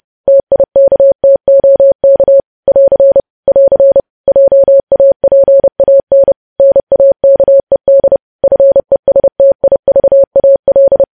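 Morse code, heard as a keyed tone, about 30 wpm, 550 Hz. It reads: TIKTOK <AR> <AR> JAPAN NAKED FESTIVAL
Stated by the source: News headlines sent at 30 wpm.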